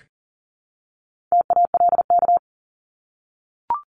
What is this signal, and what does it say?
Morse code, heard as a keyed tone, about 40 wpm, 700 Hz. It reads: TALK
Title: ING Suffix - Top 100 Words with Base Words 40wpm